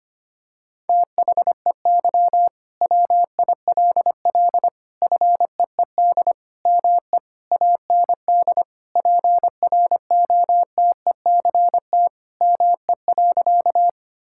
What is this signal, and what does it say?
Morse code, heard as a keyed tone, about 25 wpm, 700 Hz. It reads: THEY WILL FEED ME AND PROTECT ME.
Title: Sentences from Top 1600 Words 25wpm